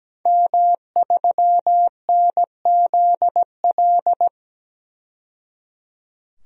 Morse code, heard as a keyed tone, about 17 wpm, 700 Hz. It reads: M3NZL
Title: DXCC-100 Call Signs - 5 Characters - 17wpm